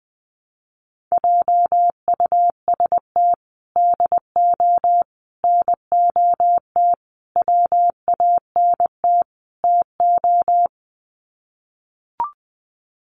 Morse code, heard as a keyed tone, about 20 wpm, 700 Hz. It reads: JUST DO NOT WANT TO
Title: Sets of 5 Words Encore 20wpm